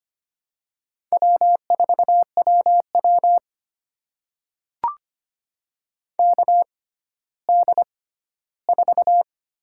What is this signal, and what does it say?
Morse code, heard as a keyed tone, about 25 wpm, spaced as for 14 wpm, 700 Hz. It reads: W4WW KD4
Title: USA Call Signs - 6x Character Spacing - 25wpm